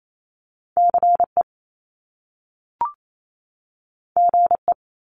Morse code, heard as a keyed tone, about 28 wpm, 700 Hz. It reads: CE GE
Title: Top Two Letter Combinations in Words 28wpm